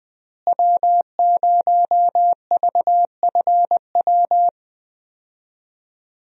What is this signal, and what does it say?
Morse code, heard as a keyed tone, about 21 wpm, 700 Hz. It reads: W0VFW
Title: USA Call Signs - 5 Characters - Repeated 2x - 20wpm